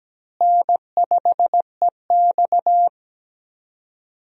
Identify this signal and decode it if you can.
Morse code, heard as a keyed tone, about 17 wpm, 700 Hz. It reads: N5EX
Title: USA Call Signs - 4 Characters - Repeated 2x - 17wpm